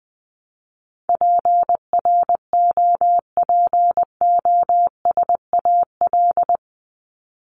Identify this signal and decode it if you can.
Morse code, heard as a keyed tone, about 20 wpm, 700 Hz. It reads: PROPOSAL
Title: Top 1751-2000 Words 20wpm